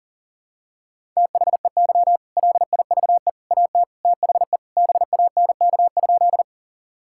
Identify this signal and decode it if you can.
Morse code, heard as a keyed tone, about 40 wpm, 700 Hz. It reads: THEY LIVE AT THE BANK?
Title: Sentences from Top 1000 Words - Review Part 6 - 40wpm